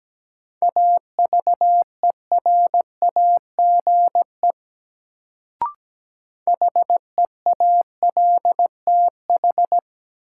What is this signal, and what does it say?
Morse code, heard as a keyed tone, about 17 wpm, 700 Hz. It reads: AVERAGE HEALTH
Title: Sets of 5 Words - New Word Drill - 17wpm